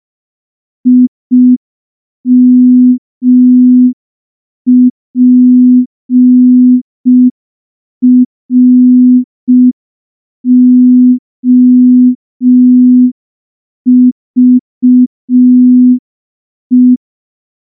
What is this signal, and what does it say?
Morse code, heard as a keyed tone, about 5 wpm, 250 Hz. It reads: IMPROVE
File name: morse / morse-training-05WPM